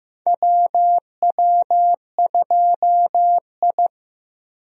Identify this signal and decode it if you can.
Morse code, heard as a keyed tone, about 16 wpm, 700 Hz. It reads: WW2I
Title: USA Call Signs - 4 Characters - Repeated 2x - 15wpm